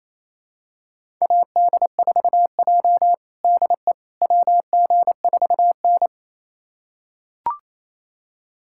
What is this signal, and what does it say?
Morse code, heard as a keyed tone, about 28 wpm, 700 Hz. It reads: AD4J DE WG4N
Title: USA Callsign DE Callsign - 4 Characters - 28wpm